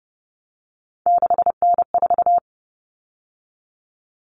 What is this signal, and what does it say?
Morse code, heard as a keyed tone, about 30 wpm, 700 Hz. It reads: 6N4
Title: Common Cut Numbers - 3-Digit 30wpm